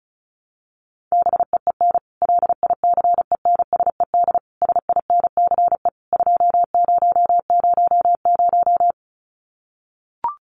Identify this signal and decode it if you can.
Morse code, heard as a keyed tone, about 34 wpm, 700 Hz. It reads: BEEN LICENSED SINCE 2000
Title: Common QSO Phrases 35wpm